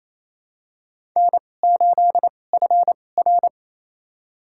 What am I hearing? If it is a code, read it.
Morse code, heard as a keyed tone, about 28 wpm, spaced as for 20 wpm, 700 Hz. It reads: N8FR
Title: USA Call Signs - 4 Characters - 2x Character Spacing - 28wpm